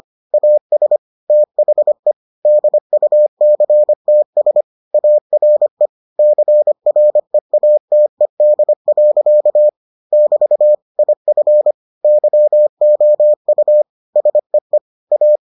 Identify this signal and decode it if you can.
Morse code, heard as a keyed tone, about 25 wpm, 600 Hz. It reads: AS THE DUCTS ARE CREATED. <BT> IF YOU SEE A